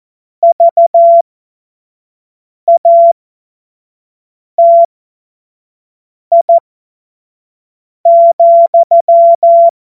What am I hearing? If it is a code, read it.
Morse code, heard as a keyed tone, about 14 wpm, spaced as for 5 wpm, 680 Hz. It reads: VATI,
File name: Morsekurs / Witze / Witz 14. September 2021, 25 F70